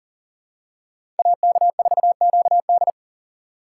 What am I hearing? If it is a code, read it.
Morse code, heard as a keyed tone, about 40 wpm, 700 Hz. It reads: AK4QD
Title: USA Call Signs - 5 Characters - 4x Character Spacing - 40wpm